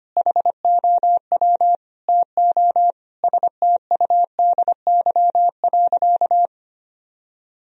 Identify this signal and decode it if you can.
Morse code, heard as a keyed tone, about 25 wpm, 700 Hz. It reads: HOW TO STUDY.
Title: Sentences from Top 1000 Words - Review Part 1 - 25wpm